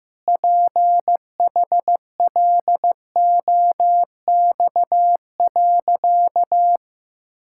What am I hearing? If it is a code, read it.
Morse code, heard as a keyed tone, about 15 wpm, 700 Hz. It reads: PHLOX.
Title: Pangram Challenge v2 - 15wpm